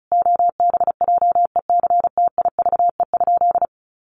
Morse code, heard as a keyed tone, about 34 wpm, 700 Hz. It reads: OBJECTIVE?
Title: Sentences from Top 1200 Words 35wpm